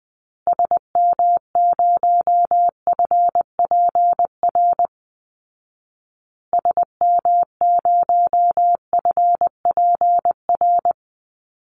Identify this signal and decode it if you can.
Morse code, heard as a keyed tone, about 20 wpm, 700 Hz. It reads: SM0FPR SM0FPR
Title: Call Signs Repeated 2x - 20wpm